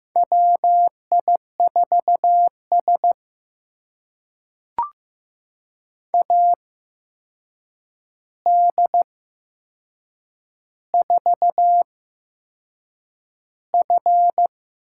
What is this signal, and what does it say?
Morse code, heard as a keyed tone, about 15 wpm, spaced as for 6 wpm, 700 Hz. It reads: WI4S AD4F
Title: USA Call Signs - 4 Characters - 8x Character Spacing - 15wpm